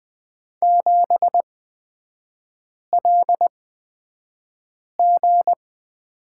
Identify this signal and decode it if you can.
Morse code, heard as a keyed tone, about 20 wpm, spaced as for 5 wpm, 690 Hz. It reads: 7LG
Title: 229 Lesson 37e - Letter x - Three Characters 20wpm